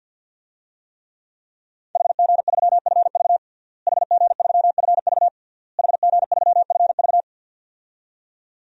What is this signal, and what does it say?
Morse code, heard as a keyed tone, about 49 wpm, 700 Hz. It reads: HG3FV HG3FV HG3FV
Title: DXCC-100 Call Signs - 5 Characters - Repeated 3x - 50wpm